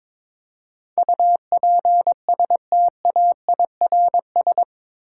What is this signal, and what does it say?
Morse code, heard as a keyed tone, about 22 wpm, 700 Hz. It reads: UPSTAIRS
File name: practice / 22wpm / Eight Letter Words - 22wpm